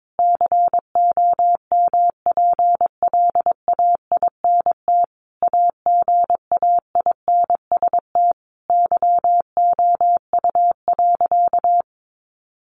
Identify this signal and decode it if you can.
Morse code, heard as a keyed tone, about 22 wpm, 700 Hz. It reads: COMPLAINT AGAINST YOU.
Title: Sentences from Top 1400 Words 22wpm